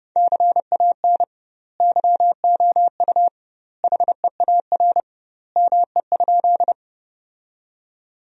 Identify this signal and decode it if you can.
Morse code, heard as a keyed tone, about 30 wpm, 700 Hz. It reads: CAN YOU HEAR ME?
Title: 2x Word-Spacing - Sentences from Top 300 Words - 30wpm